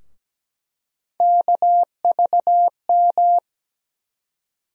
Morse code, heard as a keyed tone, about 17 wpm, 700 Hz. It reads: KVM
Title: USA Call Sign Postfixes - 3 Letters 17wpm